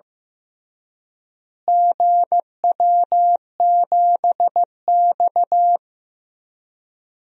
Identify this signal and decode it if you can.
Morse code, heard as a keyed tone, about 15 wpm, 700 Hz. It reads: GW7X
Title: DXCC-100 Call Signs - 4 Characters - Repeated 3x - 15wpm